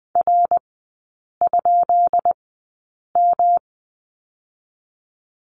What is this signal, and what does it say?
Morse code, heard as a keyed tone, about 20 wpm, spaced as for 8 wpm, 690 Hz. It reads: R?M
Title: 133 Lesson 22e - Question Mark - Three Characters 20wpm